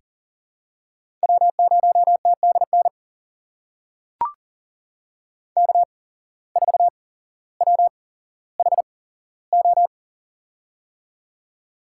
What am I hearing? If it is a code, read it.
Morse code, heard as a keyed tone, about 40 wpm, spaced as for 17 wpm, 700 Hz. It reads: W0TDN K4WHO
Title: USA Call Signs - 5 Characters - 8x Character Spacing - 40wpm